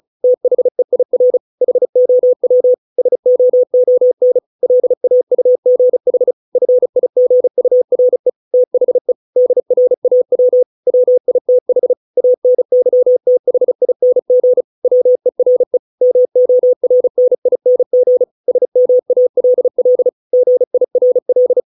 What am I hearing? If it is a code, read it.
Morse code, heard as a keyed tone, about 34 wpm, 500 Hz. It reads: THEIR HOW SOON LAUGH FIGURE THE DRAW WITH ANYTHING WERE MORNING SMALL GIRL